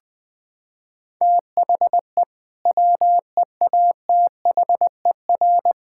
Morse code, heard as a keyed tone, about 20 wpm, 700 Hz. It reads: THE WEATHER
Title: Sentences from Top 1000 Words - Review Part 5 - 20wpm